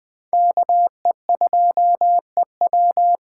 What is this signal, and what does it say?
Morse code, heard as a keyed tone, about 21 wpm, 700 Hz. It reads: KE2EW
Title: USA Call Signs - 2x Character Spacing - 20wpm